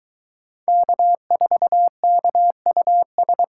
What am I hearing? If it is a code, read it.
Morse code, heard as a keyed tone, about 22 wpm, 700 Hz. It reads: K4KUS